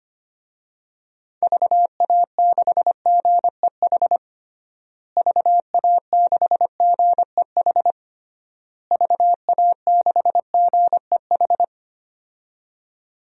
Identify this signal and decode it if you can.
Morse code, heard as a keyed tone, about 25 wpm, 700 Hz. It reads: VA6GEH VA6GEH VA6GEH
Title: DXCC-100 Call Signs - 6 Characters - Repeated 3x - 25wpm